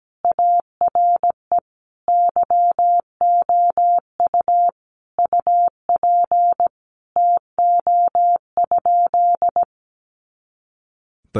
Morse code, heard as a keyed tone, about 17 wpm, 700 Hz. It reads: ARE YOU UP TO?
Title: Sentences from Top 100 Words 17wpm